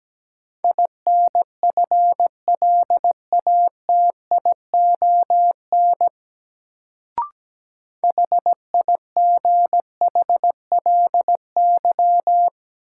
Morse code, heard as a keyed tone, about 17 wpm, 700 Hz. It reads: INFLATION HIGHLY
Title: Sentences from Top 1100 Words - New Word Drill - 17wpm